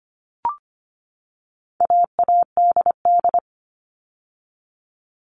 Morse code, heard as a keyed tone, about 25 wpm, 700 Hz. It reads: AADD
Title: All Cut Numbers - 4-Digit 25wpm